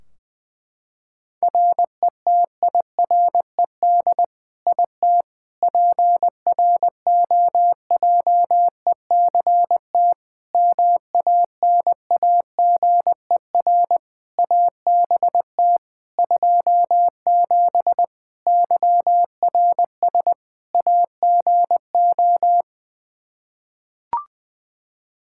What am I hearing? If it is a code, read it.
Morse code, heard as a keyed tone, about 20 wpm, 700 Hz. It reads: RETIRED IT PROJECT MANAGER ABT 27 YRS AGO
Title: Practice QSO - Exchange 3 - More Info - 20wpm